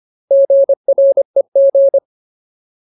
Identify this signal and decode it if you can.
Morse code, heard as a keyed tone, about 25 wpm, 550 Hz. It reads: GREG
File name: Ultimate Word List / 25WPM Ultimate